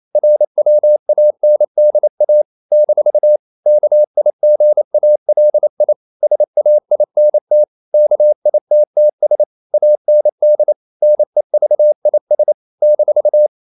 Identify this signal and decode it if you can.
Morse code, heard as a keyed tone, about 28 wpm, 600 Hz. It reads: RWANDA - KIGALI SAINT KITTS AND NEVIS -
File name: CW / Practice / NationalCapitals28